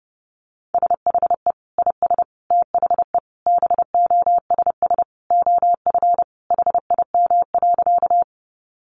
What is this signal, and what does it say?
Morse code, heard as a keyed tone, about 30 wpm, 700 Hz. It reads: SHE IS THE BOSS OF HIM.
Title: Sentences from Top 700 Words 30wpm